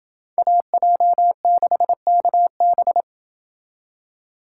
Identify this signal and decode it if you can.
Morse code, heard as a keyed tone, about 29 wpm, 700 Hz. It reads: AJ6KB